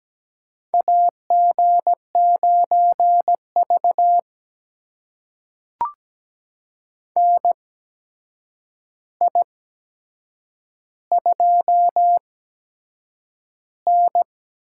Morse code, heard as a keyed tone, about 17 wpm, spaced as for 7 wpm, 700 Hz. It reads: AG9V NI2N
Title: USA Call Signs - 4 Characters - 8x Character Spacing - 17wpm